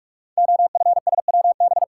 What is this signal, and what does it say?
Morse code, heard as a keyed tone, about 44 wpm, 700 Hz. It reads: OFSWB
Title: Code Groups - 5 Letters - 45wpm